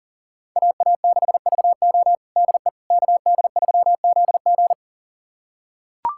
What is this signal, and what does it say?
Morse code, heard as a keyed tone, about 40 wpm, 700 Hz. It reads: AA6VO DE KD3ZG
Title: USA Callsign DE Callsign - 5 Characters - 40wpm